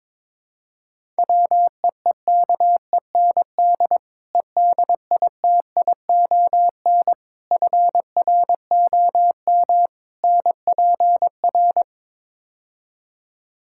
Morse code, heard as a keyed tone, about 22 wpm, 700 Hz. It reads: WEEKEND EDITION FROM NPR
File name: practice / 22wpm / Build A Phrase - 5-Word Phrases - 22wpm